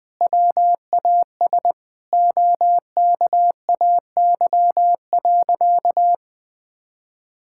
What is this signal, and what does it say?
Morse code, heard as a keyed tone, about 20 wpm, 700 Hz. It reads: WAS OKAY.
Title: Sentences with 2-Word Phrases - Spoken Beginning - 20wpm